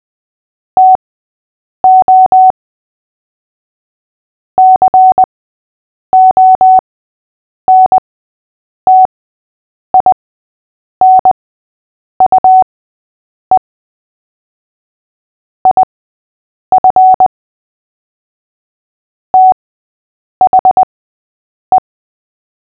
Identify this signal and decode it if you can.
Morse code, heard as a keyed tone, about 20 wpm, spaced as for 8 wpm, 750 Hz. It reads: TO CONTINUE IF THE
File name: cw / fists / Short-Story-8-8wpm